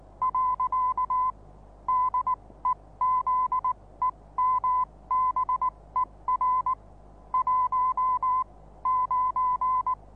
Morse code, heard as a keyed tone, about 19 wpm, spaced as for 16 wpm, 1010 Hz. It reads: . DEZEMBER 1 9
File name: repos / Morsekurs / DJ4UF Morsen26